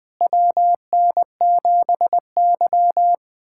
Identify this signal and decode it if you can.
Morse code, heard as a keyed tone, about 20 wpm, 700 Hz. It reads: WN7Y
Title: USA Callsign DE Callsign - 4 Characters - 20wpm